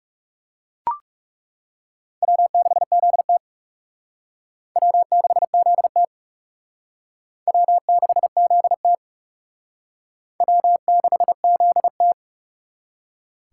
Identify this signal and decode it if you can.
Morse code, heard as a keyed tone, about 38 wpm, 700 Hz. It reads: W6ZT W6ZT W6ZT W6ZT